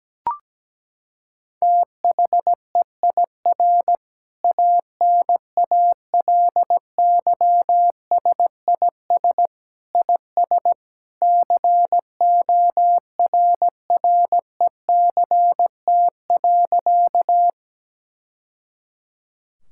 Morse code, heard as a keyed tone, about 17 wpm, 700 Hz. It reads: THEIR ANALYSIS IS CORRECT.
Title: Sentences from Top 1000 Words - Review Part 5 - 17wpm